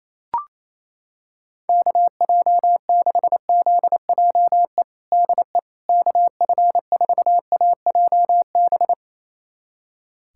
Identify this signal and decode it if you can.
Morse code, heard as a keyed tone, about 28 wpm, 700 Hz. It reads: KJ6ZJE DE KF4AJB
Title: USA Callsign DE Callsign 28wpm